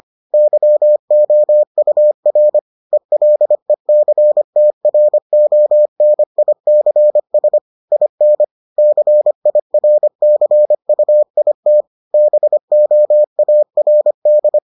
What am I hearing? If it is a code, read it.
Morse code, heard as a keyed tone, about 25 wpm, 600 Hz. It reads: YOUR ELECTRONICS IN CIRCUIT BOARD